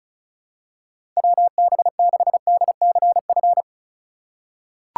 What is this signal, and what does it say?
Morse code, heard as a keyed tone, about 36 wpm, 700 Hz. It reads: WB6DCF
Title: USA Call Signs - 6 Characters - 3x Character Spacing - 35wpm